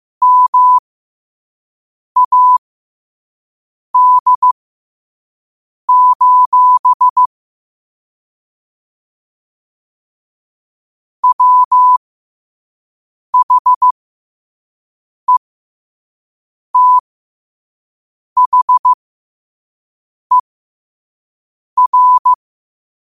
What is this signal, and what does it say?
Morse code, heard as a keyed tone, about 15 wpm, spaced as for 5 wpm, 1000 Hz. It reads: MAD: WHETHER
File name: qotdfiles / 5wpm